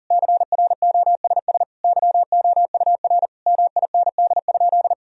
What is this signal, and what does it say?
Morse code, heard as a keyed tone, about 40 wpm, 700 Hz. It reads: CROSS YOUR MIND?
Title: Sentences from Top 1000 Words - Review Part 1 - 40wpm